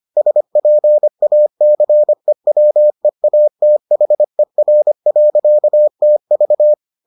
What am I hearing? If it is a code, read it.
Morse code, heard as a keyed tone, about 25 wpm, 600 Hz. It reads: SPACEWEATHER.TV